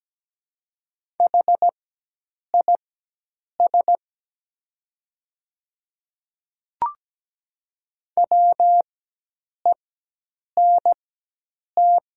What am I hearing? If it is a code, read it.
Morse code, heard as a keyed tone, about 17 wpm, spaced as for 8 wpm, 700 Hz. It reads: HIS WENT
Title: 4x Character Spacing - Top 200 Words - 17wpm